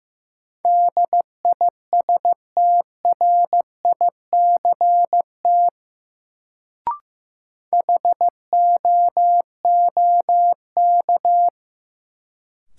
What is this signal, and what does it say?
Morse code, heard as a keyed tone, about 15 wpm, 700 Hz. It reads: DISTRICT HOOK
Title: Sentences from Top 1600 Words - New Word Drill - 15wpm